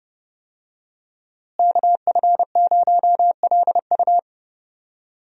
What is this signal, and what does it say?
Morse code, heard as a keyed tone, about 30 wpm, 700 Hz. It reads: KF0LU
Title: USA Call Signs - 5 Characters - Repeated 3x - 30wpm